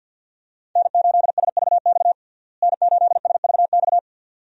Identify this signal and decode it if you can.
Morse code, heard as a keyed tone, about 52 wpm, 700 Hz. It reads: N8SVX N8SVX
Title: USA Call Signs - 5 Characters - Repeated 2x - 50wpm